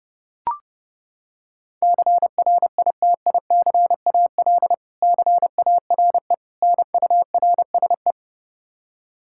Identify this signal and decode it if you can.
Morse code, heard as a keyed tone, about 29 wpm, 700 Hz. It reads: CRITICAL CARE NURSE
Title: K1USN SST QSO Practice 30wpm